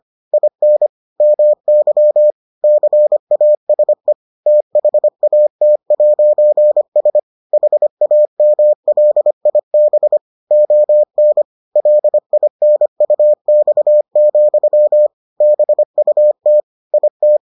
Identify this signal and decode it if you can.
Morse code, heard as a keyed tone, about 25 wpm, 600 Hz. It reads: IN MY CASE THAT'S HAMLIB ON LINUX, BUT IT